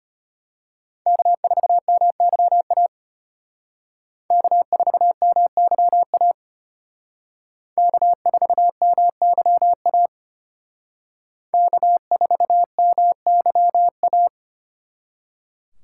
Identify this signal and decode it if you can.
Morse code, heard as a keyed tone, about 33 wpm, 700 Hz. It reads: K4MYA K4MYA K4MYA K4MYA